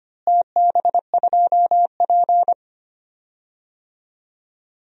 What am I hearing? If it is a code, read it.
Morse code, heard as a keyed tone, about 25 wpm, 700 Hz. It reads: TB2P
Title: DXCC-100 Call Signs - 4 Characters - Repeated 2x - 25wpm